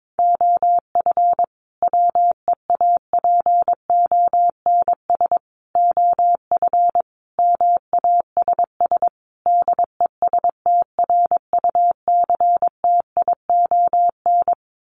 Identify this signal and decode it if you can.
Morse code, heard as a keyed tone, about 22 wpm, 700 Hz. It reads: OF WEAPONS OF MASS DESTRUCTION
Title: Sets of 5 Words Encore 22wpm